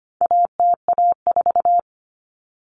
Morse code, heard as a keyed tone, about 25 wpm, 700 Hz. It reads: ATA4
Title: Common Cut Numbers - 4-Digit 25wpm